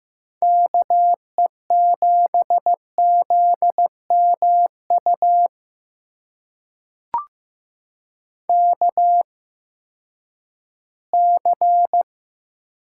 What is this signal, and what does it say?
Morse code, heard as a keyed tone, about 15 wpm, spaced as for 10 wpm, 700 Hz. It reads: KE7ZMU KC
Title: USA Call Signs - 8x Character Spacing - 15wpm